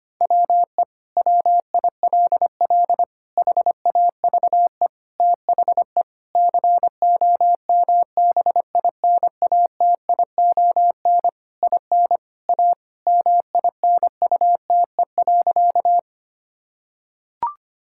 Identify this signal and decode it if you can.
Morse code, heard as a keyed tone, about 25 wpm, 700 Hz. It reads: WE WILL HAVE THE COMBINATION IN A MINUTE.
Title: Sentences from Top 1000 Words - Review Part 6 - 25wpm